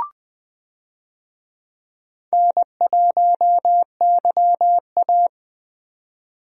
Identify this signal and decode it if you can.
Morse code, heard as a keyed tone, about 20 wpm, 700 Hz. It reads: N1YA
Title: USA Call Signs - 4 Characters - Repeated 3x - 20wpm